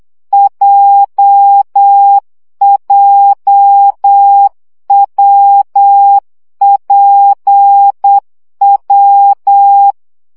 Morse code, heard as a keyed tone, about 8 wpm, 800 Hz. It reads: JJWPW
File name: CW gyakorlas w p j 7 baud